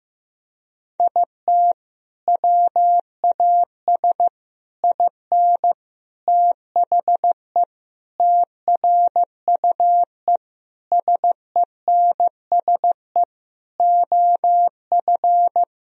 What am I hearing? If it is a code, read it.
Morse code, heard as a keyed tone, about 15 wpm, 700 Hz. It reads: IT WAS IN THE TRUE SENSE OF
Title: Sentences from Top 1000 Words - Review Part 4 - 15wpm